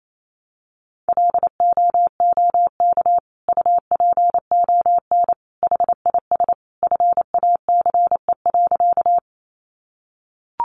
Morse code, heard as a keyed tone, about 28 wpm, 700 Hz. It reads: LOOK UPON HIS FACE.
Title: Sentences from Top 1100 Words 28wpm